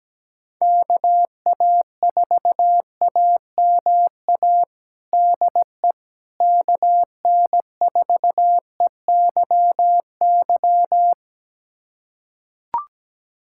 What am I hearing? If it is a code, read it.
Morse code, heard as a keyed tone, about 17 wpm, 700 Hz. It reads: KA4AMA DE KN4EYY
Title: USA Callsign DE Callsign 17wpm